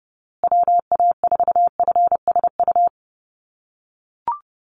Morse code, heard as a keyed tone, about 30 wpm, 700 Hz. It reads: WA4FSU
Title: USA Call Signs Repeated 2x 30wpm